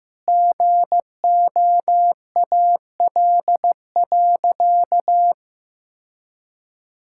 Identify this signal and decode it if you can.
Morse code, heard as a keyed tone, about 15 wpm, 700 Hz. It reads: GOAL.
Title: Sentences from Top 1000 Words - Review Part 5 - 15wpm